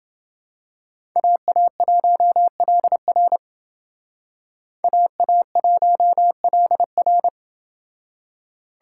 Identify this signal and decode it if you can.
Morse code, heard as a keyed tone, about 28 wpm, 700 Hz. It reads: AA1LR AA1LR